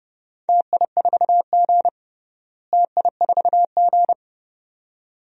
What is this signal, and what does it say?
Morse code, heard as a keyed tone, about 30 wpm, 700 Hz. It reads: TI4G TI4G
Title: DXCC-100 Call Signs - 4 Characters - Repeated 2x - 30wpm